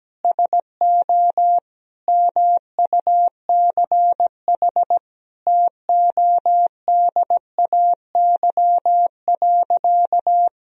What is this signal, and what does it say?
Morse code, heard as a keyed tone, about 17 wpm, 700 Hz. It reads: SO MUCH TODAY.
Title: Sentences from Top 1000 Words - Review Part 4 - 17wpm